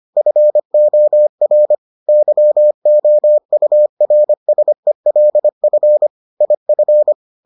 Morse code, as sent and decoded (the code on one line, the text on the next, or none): ..-. --- .-. / -.-- --- ..- .-. ... .-.. ..-. / .. ..-.
FOR YOURSELF IF